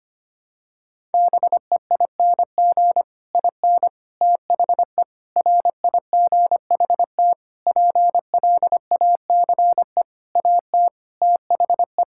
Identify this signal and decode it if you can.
Morse code, heard as a keyed tone, about 25 wpm, 700 Hz. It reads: BEING IN THE RIGHT PLACE AT THE
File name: practice / 25wpm / Sentences with 9-Word Phrases - 25wpm